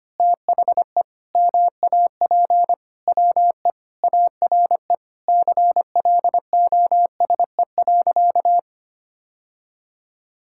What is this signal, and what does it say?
Morse code, heard as a keyed tone, about 25 wpm, 700 Hz. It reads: THE MAP WE ARE CLOSE.
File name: practice / 25wpm / Sentences with 2-Word Phrases - 25wpm